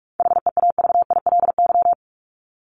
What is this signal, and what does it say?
Morse code, heard as a keyed tone, about 44 wpm, 700 Hz. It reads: HEAVILY
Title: LY Suffix - Top 100 Words with Base Words 45wpm